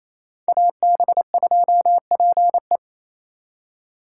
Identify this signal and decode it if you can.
Morse code, heard as a keyed tone, about 36 wpm, 700 Hz. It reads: AB2PE